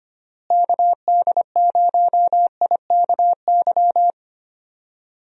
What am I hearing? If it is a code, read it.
Morse code, heard as a keyed tone, about 25 wpm, 700 Hz. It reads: KD0IKY
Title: USA Call Signs - 6 Characters - Repeated 3x - 25wpm